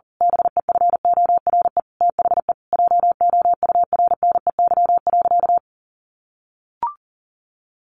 Morse code, heard as a keyed tone, about 37 wpm, 700 Hz. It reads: BEFORE THE JOURNEY.
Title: A Sound Of Thunder 40wpm